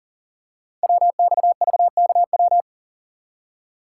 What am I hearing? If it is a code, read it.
Morse code, heard as a keyed tone, about 40 wpm, 700 Hz. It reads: WXVKW
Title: Code Groups - 5 Letters - 40wpm